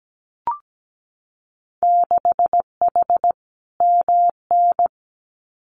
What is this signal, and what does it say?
Morse code, heard as a keyed tone, about 17 wpm, 700 Hz. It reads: 6H MN
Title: Winter Field Day Exchanges Encore 17wpm